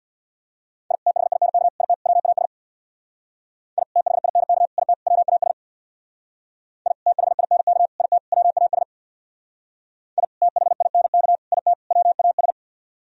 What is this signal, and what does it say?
Morse code, heard as a keyed tone, about 62 wpm, 700 Hz. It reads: I THINK IT WAS I THINK IT WAS I THINK IT WAS I THINK IT WAS
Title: Speed-Racing - Sets of 4 Words 50wpm